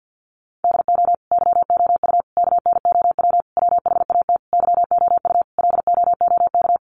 Morse code, heard as a keyed tone, about 50 wpm, 700 Hz. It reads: DO YOU KNOW WHAT YOU LOOK
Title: Sentences from Top 100 Words 50wpm